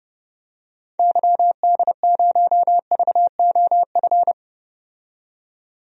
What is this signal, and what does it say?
Morse code, heard as a keyed tone, about 30 wpm, 700 Hz. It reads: YD0VOF
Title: DXCC-100 Call Signs - 6 Characters - 30wpm